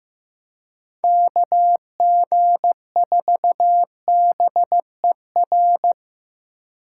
Morse code, heard as a keyed tone, about 18 wpm, 700 Hz. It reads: KG4BER
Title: Speed Racing - USA Call Signs 15wpm